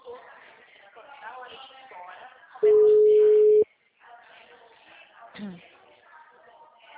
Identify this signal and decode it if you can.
Ringback tone (Europe, 425 Hz), with speech.